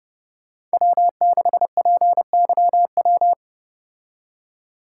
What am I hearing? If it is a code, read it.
Morse code, heard as a keyed tone, about 29 wpm, 700 Hz. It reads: W6PYW